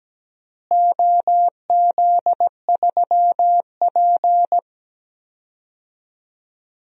Morse code, heard as a keyed tone, about 17 wpm, 700 Hz. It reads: OZ3P
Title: DXCC-100 Call Signs - 4 Characters - Repeated 3x - 17wpm